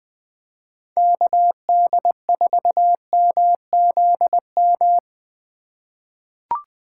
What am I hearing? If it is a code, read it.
Morse code, heard as a keyed tone, about 20 wpm, 700 Hz. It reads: KD4MZM
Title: 2023 POTA Activator Call Signs - 20wpm